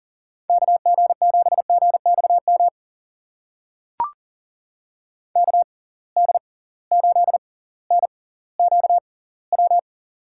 Morse code, heard as a keyed tone, about 40 wpm, spaced as for 20 wpm, 700 Hz. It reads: KG7GXM KD8NQW
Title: USA Call Signs - 6 Characters - 6x Character Spacing - 40wpm